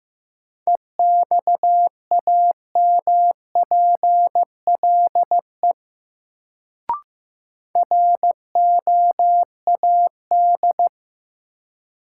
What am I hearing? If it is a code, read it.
Morse code, heard as a keyed tone, about 15 wpm, 700 Hz. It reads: EXAMPLE ROAD
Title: Sentences from Top 400 Words - New Word Drill - 15wpm